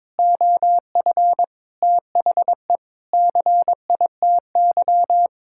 Morse code, heard as a keyed tone, about 22 wpm, 700 Hz. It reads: OF THE CITY
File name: practice / 22wpm / Sentences with 3-Word Phrases - 22wpm